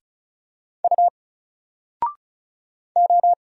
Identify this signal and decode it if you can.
Morse code, heard as a keyed tone, about 33 wpm, 700 Hz. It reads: U O
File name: Single-Letter-Number-35wpm